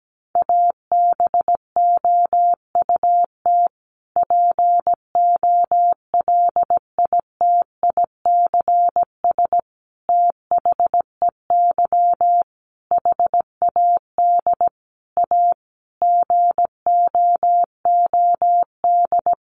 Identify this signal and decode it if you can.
Morse code, heard as a keyed tone, about 17 wpm, 700 Hz. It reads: ABOUT POLITICS THEY HAD A GOOD
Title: Sentences from Top 1000 Words 17wpm